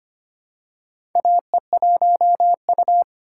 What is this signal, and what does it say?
Morse code, heard as a keyed tone, about 25 wpm, 700 Hz. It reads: AE1U
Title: USA Call Signs - 4x Character Spacing - 25wpm